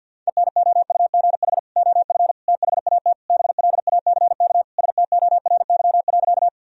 Morse code, heard as a keyed tone, about 50 wpm, 700 Hz. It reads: ENOUGH OF THAT BLACK STORY.
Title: Sentences from Top 1000 Words - Review Part 2 - 50wpm